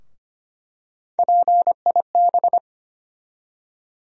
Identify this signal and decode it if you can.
Morse code, heard as a keyed tone, about 25 wpm, 700 Hz. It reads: PIB
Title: USA Call Sign Postfixes - 3 Letters 25wpm